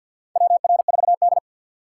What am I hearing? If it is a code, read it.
Morse code, heard as a keyed tone, about 52 wpm, 700 Hz. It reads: WR4D
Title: USA Call Signs - 4 Characters - 6x Character Spacing - 50wpm